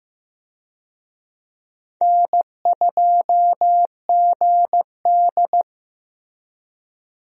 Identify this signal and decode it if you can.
Morse code, heard as a keyed tone, about 15 wpm, 700 Hz. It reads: N2GD
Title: USA Call Signs - 4 Characters - Repeated 2x - 15wpm